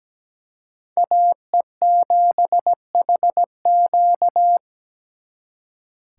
Morse code, heard as a keyed tone, about 20 wpm, 700 Hz. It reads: AE7HQ
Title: Speed Racing - USA 2x2 Call Signs 17wpm